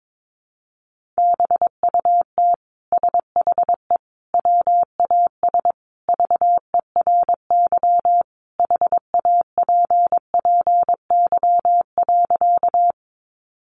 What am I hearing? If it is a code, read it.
Morse code, heard as a keyed tone, about 22 wpm, 700 Hz. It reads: BUT SHE WAS VERY HAPPY.
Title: Sentences from Top 1000 Words 22wpm